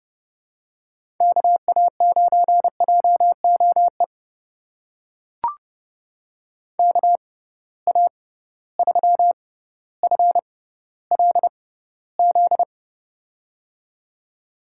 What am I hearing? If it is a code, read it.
Morse code, heard as a keyed tone, about 30 wpm, spaced as for 15 wpm, 700 Hz. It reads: KA9JOE KA3FLZ
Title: USA Call Signs - 6 Characters - 6x Character Spacing - 30wpm